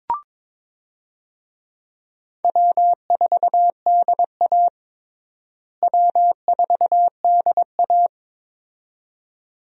Morse code, heard as a keyed tone, about 22 wpm, 700 Hz. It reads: W4DA W4DA
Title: USA Call Signs - 4 Characters - Repeated 2x - 22wpm